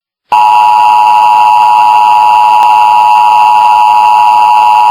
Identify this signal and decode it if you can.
Emergency Alert System (SAME) tones (the 853 and 960 Hz attention tone), with noise. It is heard from 0.3 to 4.7 s.